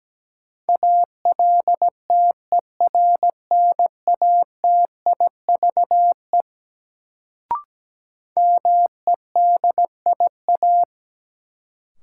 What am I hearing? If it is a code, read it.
Morse code, heard as a keyed tone, about 17 wpm, 700 Hz. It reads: ALTERNATIVE MEDIA
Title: Sentences from Top 1000 Words - Review Part 4 - New Word Drill - 17wpm